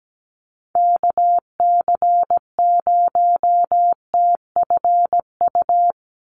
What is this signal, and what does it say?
Morse code, heard as a keyed tone, about 17 wpm, 700 Hz. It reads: KC0TFU
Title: USA Call Signs 17wpm